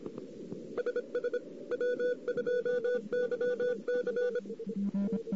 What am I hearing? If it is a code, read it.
Morse code, heard as a keyed tone, about 26 wpm, 500 Hz. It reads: SS W2YC
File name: CW / 2008-ss-cw-k5tr-op-n5rz -14:00